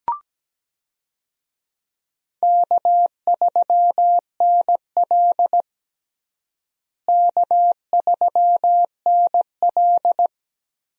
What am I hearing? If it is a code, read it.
Morse code, heard as a keyed tone, about 17 wpm, 700 Hz. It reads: K3NL K3NL
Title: USA Call Signs - 4 Characters - Repeated 3x - 17wpm